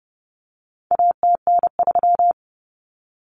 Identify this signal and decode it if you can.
Morse code, heard as a keyed tone, about 30 wpm, 700 Hz. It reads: ATN3
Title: Common Cut Numbers - 4-Digit 30wpm